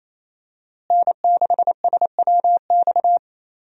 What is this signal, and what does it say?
Morse code, heard as a keyed tone, about 27 wpm, 700 Hz. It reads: N6SWX
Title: USA Call Signs - 5 Characters - Repeated 2x - 28wpm